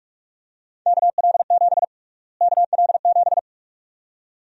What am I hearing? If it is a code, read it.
Morse code, heard as a keyed tone, about 45 wpm, 700 Hz. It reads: KL7 KL7
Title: USA Call Sign Prefixes - Repeated 2x - 45wpm